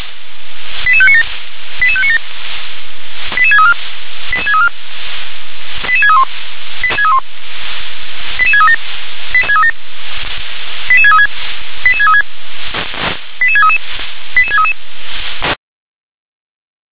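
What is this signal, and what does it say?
Five-tone selective call (ZVEI), heard as quick runs of five tones, tones of about 70 ms. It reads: 88597 88597 88634 88634 88631 88637 88636 88636 88633 88633
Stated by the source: ZVEI five-tone selective call received on SDR.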